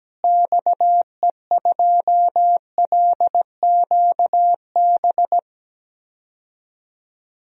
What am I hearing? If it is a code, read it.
Morse code, heard as a keyed tone, about 17 wpm, 700 Hz. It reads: XE2LQB
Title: DXCC-100 Call Signs - 6 Characters - Repeated 3x - 17wpm